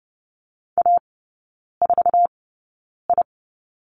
Morse code, heard as a keyed tone, about 30 wpm, spaced as for 9 wpm, 690 Hz. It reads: A4I
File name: Lesson 16e - Letter c - Three Characters 30wpm